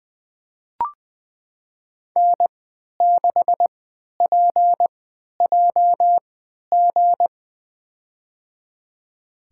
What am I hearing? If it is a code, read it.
Morse code, heard as a keyed tone, about 20 wpm, spaced as for 11 wpm, 700 Hz. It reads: N6PJG
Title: USA Call Signs - 5 Characters - 3x Character Spacing - 20wpm